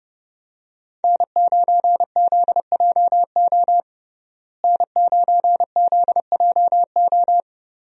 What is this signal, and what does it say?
Morse code, heard as a keyed tone, about 32 wpm, 700 Hz. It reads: N9ZJO N9ZJO
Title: USA Call Signs - 5 Characters - Repeated 3x - 30wpm